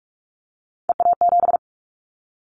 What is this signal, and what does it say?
Morse code, heard as a keyed tone, about 46 wpm, 700 Hz. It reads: EA7
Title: Callsign Prefixes for Top 25 DXCC - 45wpm